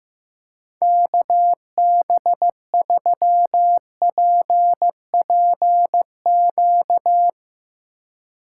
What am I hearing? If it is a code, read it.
Morse code, heard as a keyed tone, about 15 wpm, 700 Hz. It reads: KB3PPQ
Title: USA Call Signs - 6 Characters - 2x Character Spacing - 15wpm